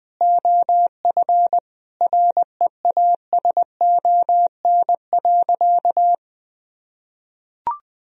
Morse code, heard as a keyed tone, about 20 wpm, 700 Hz. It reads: OF REASON.
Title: Sentences from Top 1000 Words - Review Part 3 - 20wpm